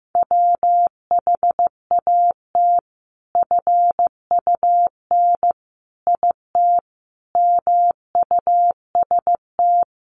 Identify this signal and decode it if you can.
Morse code, heard as a keyed tone, about 15 wpm, 700 Hz. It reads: WHAT FUN IT MUST
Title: Sentences from Top 700 Words 15wpm